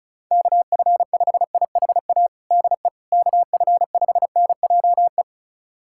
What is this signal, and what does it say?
Morse code, heard as a keyed tone, about 36 wpm, 700 Hz. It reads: KF5IHA DE KF5NJE
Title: USA Callsign DE Callsign 35wpm